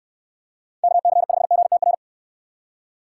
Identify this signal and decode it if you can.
Morse code, heard as a keyed tone, about 65 wpm, 700 Hz. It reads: KC5GTR